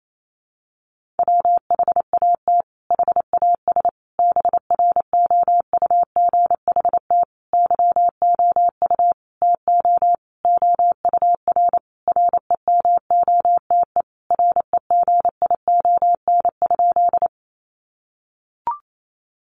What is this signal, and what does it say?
Morse code, heard as a keyed tone, about 28 wpm, 700 Hz. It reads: WHAT HAS BROUGHT YOU TO OUR REMOTE REGION?
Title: Sentences from Top 1500 Words 28wpm